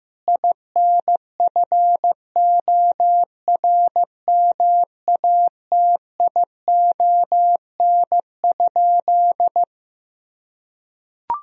Morse code, heard as a keyed tone, about 15 wpm, 700 Hz. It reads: INFORMATION?
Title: Sentences from Top 1000 Words - Review Part 3 - 15wpm